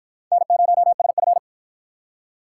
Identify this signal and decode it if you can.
Morse code, heard as a keyed tone, about 53 wpm, 700 Hz. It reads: N0SF